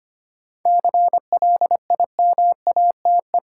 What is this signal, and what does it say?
Morse code, heard as a keyed tone, about 25 wpm, 700 Hz. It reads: CLIMATE
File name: practice / 25wpm / Difficult Sentences with Hints - 25wpm